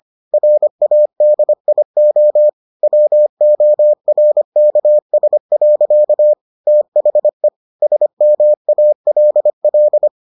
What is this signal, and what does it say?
Morse code, heard as a keyed tone, about 25 wpm, 600 Hz. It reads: RADIO WORKS. THE SMALL